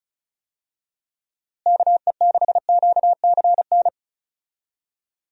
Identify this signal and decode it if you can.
Morse code, heard as a keyed tone, about 36 wpm, 700 Hz. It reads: KE6QCN
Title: USA Call Signs - 6 Characters - Repeated 2x - 35wpm